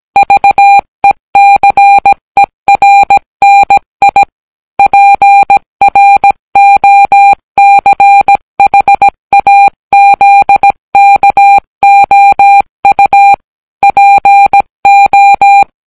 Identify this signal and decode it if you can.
Morse code, heard as a keyed tone, about 17 wpm, 800 Hz. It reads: VECERNI PROCHAZKOU PO